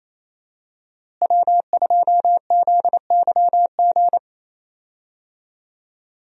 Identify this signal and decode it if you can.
Morse code, heard as a keyed tone, about 27 wpm, 700 Hz. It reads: W2ZYG
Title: USA Call Signs - 5 Characters - Repeated 2x - 28wpm